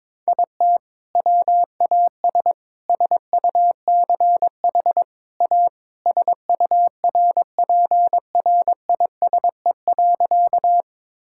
Morse code, heard as a keyed tone, about 22 wpm, 700 Hz. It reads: IT WAS SUCH A SURPRISE.
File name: practice / 22wpm / Sentences with 2-Word Phrases - 22wpm